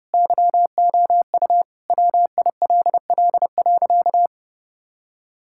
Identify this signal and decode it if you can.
Morse code, heard as a keyed tone, about 30 wpm, 700 Hz. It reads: YOU WILL.
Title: Sentences from Top 1600 Words 30wpm